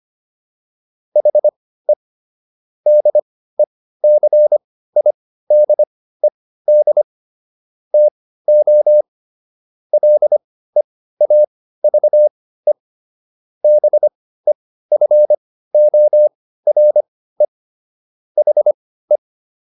Morse code, heard as a keyed tone, about 25 wpm, spaced as for 15 wpm, 600 Hz. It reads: HE DECIDED TO LEAVE BEFORE HE